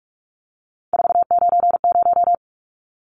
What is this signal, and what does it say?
Morse code, heard as a keyed tone, about 44 wpm, 700 Hz. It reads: 490
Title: Numbers - 3-Digit 45wpm